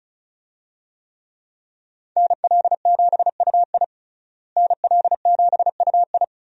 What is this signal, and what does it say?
Morse code, heard as a keyed tone, about 34 wpm, 700 Hz. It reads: NL7UI NL7UI
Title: USA Call Signs - 5 Characters - Repeated 2x - 35wpm